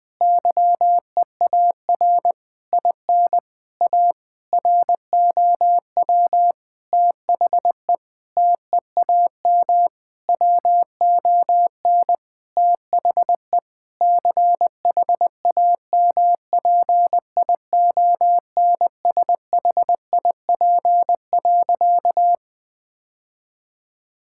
Morse code, heard as a keyed tone, about 20 wpm, 700 Hz. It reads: YEAR IN A ROW THE TEAM WON THE CHAMPIONSHIP.
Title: Sentences with 8-Word Phrases - 20wpm